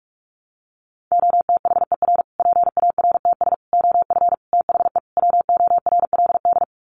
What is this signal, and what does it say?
Morse code, heard as a keyed tone, about 44 wpm, 700 Hz. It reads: OTHER PARTS OF THE WORLD
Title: Sets of 5 Words Encore 45wpm